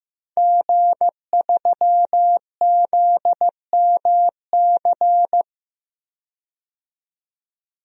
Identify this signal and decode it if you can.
Morse code, heard as a keyed tone, about 15 wpm, 700 Hz. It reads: G3ZMC
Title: DXCC-100 Call Signs - 5 Characters - 15wpm